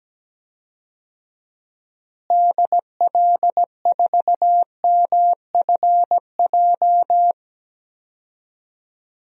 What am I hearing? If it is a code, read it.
Morse code, heard as a keyed tone, about 17 wpm, 700 Hz. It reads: DL4MFJ
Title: DXCC-100 Call Signs - 6 Characters - Repeated 3x - 17wpm